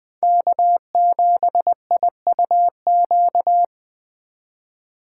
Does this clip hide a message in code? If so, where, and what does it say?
Yes: Morse code from 0.2 to 3.6 s, heard as a keyed tone, about 25 wpm, 700 Hz. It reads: K7IUQ